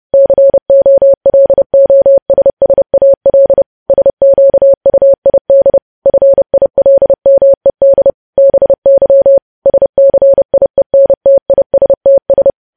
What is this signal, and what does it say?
Morse code, heard as a keyed tone, about 30 wpm, 550 Hz. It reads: COLOSSAL SQUID FILMED BY SCIENTISTS
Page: news headlines in morse code at 30 wpm